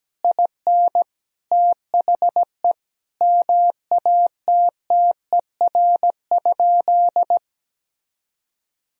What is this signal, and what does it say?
Morse code, heard as a keyed tone, about 17 wpm, 700 Hz. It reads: IN THE MATTER?
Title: Sentences from Top 1000 Words - Review Part 1 - 17wpm